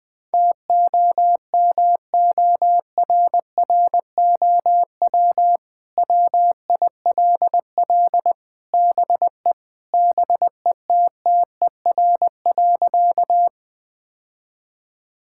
Morse code, heard as a keyed tone, about 21 wpm, 700 Hz. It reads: TOMORROW WILL BE BETTER.
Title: Sentences with 2-Word Phrases - Spoken Beginning - 20wpm